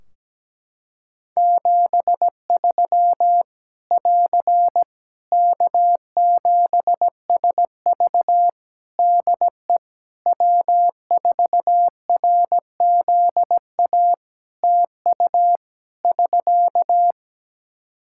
Morse code, heard as a keyed tone, about 17 wpm, 700 Hz. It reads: 73 <AR> K7SV DE W4RZA TU <SK>
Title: Practice QSO - Exchange 4 - Ending - 17wpm